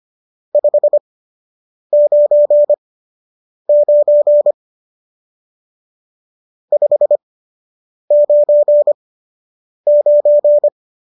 Morse code, heard as a keyed tone, about 25 wpm, spaced as for 8 wpm, 590 Hz. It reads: 599 599